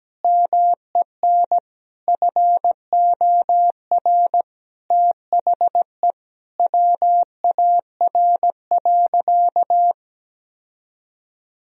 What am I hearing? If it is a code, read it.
Morse code, heard as a keyed tone, about 17 wpm, 700 Hz. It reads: MEN FOR THE WAR.
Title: Sentences from Top 1000 Words - Review Part 4 - 17wpm